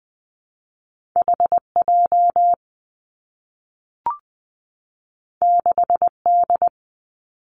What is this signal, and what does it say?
Morse code, heard as a keyed tone, about 19 wpm, 700 Hz. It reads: H J 6 D
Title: Two Challenging Letter-Number Combinations 20wpm